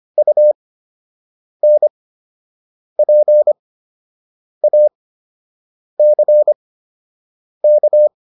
Morse code, heard as a keyed tone, about 25 wpm, spaced as for 7 wpm, 600 Hz. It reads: UNPACK